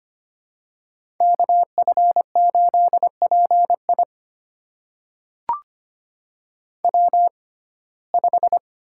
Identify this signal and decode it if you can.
Morse code, heard as a keyed tone, about 25 wpm, spaced as for 18 wpm, 700 Hz. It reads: KF8PI W5
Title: USA Call Signs - 5 Characters - 6x Character Spacing - 25wpm